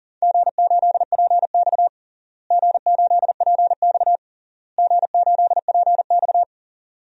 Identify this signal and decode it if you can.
Morse code, heard as a keyed tone, about 39 wpm, 700 Hz. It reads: G8PX G8PX G8PX
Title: DXCC-100 Call Signs - 4 Characters - Repeated 3x - 40wpm